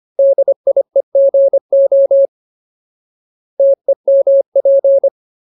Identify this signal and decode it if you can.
Morse code, heard as a keyed tone, about 25 wpm, 550 Hz. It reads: DIEGO TEMP
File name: Ultimate Word List / 25WPM Ultimate